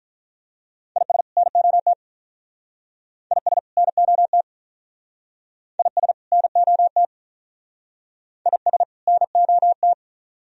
Morse code, heard as a keyed tone, about 46 wpm, 700 Hz. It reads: IS NOT IS NOT IS NOT IS NOT